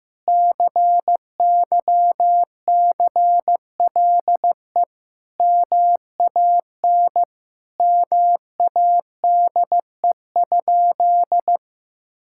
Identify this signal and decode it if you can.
Morse code, heard as a keyed tone, about 15 wpm, 700 Hz. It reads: CYCLE MAN MADE?
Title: Sentences from Top 1000 Words - Review Part 6 - 15wpm